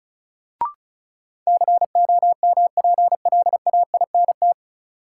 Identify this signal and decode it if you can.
Morse code, heard as a keyed tone, about 36 wpm, 700 Hz. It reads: COMPLAINT
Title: Sentences from Top 1400 Words - New Word Drill - 35wpm